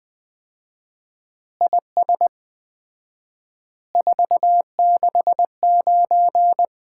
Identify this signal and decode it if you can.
Morse code, heard as a keyed tone, about 20 wpm, 700 Hz. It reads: IS 469
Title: Common QSO Phrases - 4x Word-Spacing - 20wpm